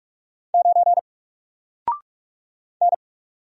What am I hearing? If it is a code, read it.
Morse code, heard as a keyed tone, about 44 wpm, 700 Hz. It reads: 9 N
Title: Single Character (Letter:Number:Punctuation:Prosign) 45wpm